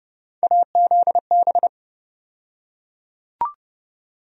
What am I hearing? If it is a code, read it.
Morse code, heard as a keyed tone, about 32 wpm, 700 Hz. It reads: AZB
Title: USA Call Sign Postfixes - 3 Letters 30wpm